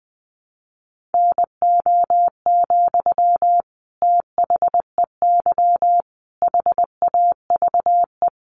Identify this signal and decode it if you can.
Morse code, heard as a keyed tone, about 20 wpm, 700 Hz. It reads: NO, THEY HAVE
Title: Sentences from Top 1300 Words 20wpm